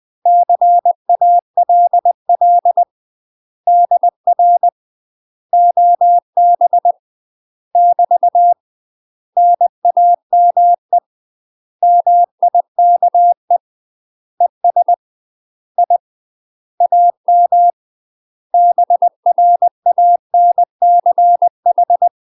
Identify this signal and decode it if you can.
Morse code, heard as a keyed tone, about 20 wpm, 700 Hz. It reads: CALL DR OB <BT> NAME MIKE ES I AM BRANCH
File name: UK MASTER QSOs TRAINING / 20WPM / UK MASTER QSOs 0026